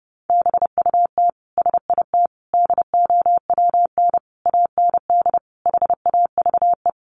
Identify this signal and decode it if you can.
Morse code, heard as a keyed tone, about 30 wpm, 700 Hz. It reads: BUT SIT DOWN AND HAVE
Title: Sentences from Top 600 Words 30wpm